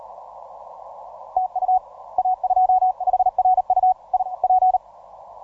Morse code, heard as a keyed tone, about 41 wpm, 740 Hz, with radio noise. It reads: TU A2 HRU I P